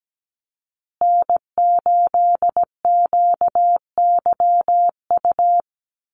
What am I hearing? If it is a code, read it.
Morse code, heard as a keyed tone, about 17 wpm, 700 Hz. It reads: N8QYU
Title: USA Call Signs Repeated 2x 17wpm